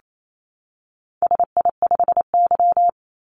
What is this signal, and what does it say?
Morse code, heard as a keyed tone, about 28 wpm, 700 Hz. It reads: SI5Y
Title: Call Signs - 28wpm